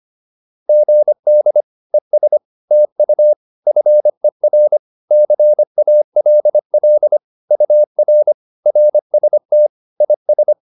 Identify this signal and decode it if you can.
Morse code, heard as a keyed tone, about 25 wpm, 600 Hz. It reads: GD ES TU FER CALL UR RST IS